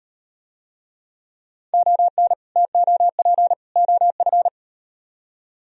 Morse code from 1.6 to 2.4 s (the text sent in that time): ON